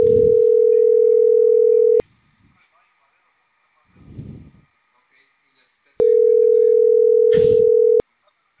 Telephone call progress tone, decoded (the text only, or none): ringback tone